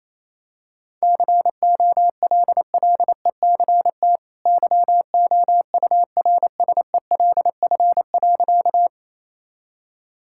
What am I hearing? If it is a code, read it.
Morse code, heard as a keyed tone, about 28 wpm, 700 Hz. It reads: COLLECT YOURSELF.
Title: Sentences from Top 1600 Words 28wpm